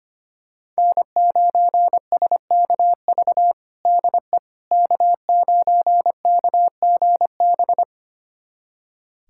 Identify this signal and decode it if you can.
Morse code, heard as a keyed tone, about 25 wpm, 700 Hz. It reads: N9SKV DE K9KGB
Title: USA Callsign DE Callsign - 5 Characters - 25wpm